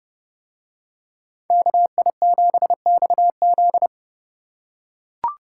Morse code, heard as a keyed tone, about 29 wpm, 700 Hz. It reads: KI7XZ